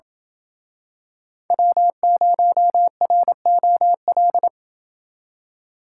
Morse code, heard as a keyed tone, about 28 wpm, 700 Hz. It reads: W0ROL